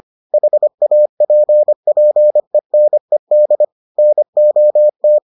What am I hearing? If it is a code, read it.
Morse code, heard as a keyed tone, about 25 wpm, 600 Hz. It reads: HAPPENED NOT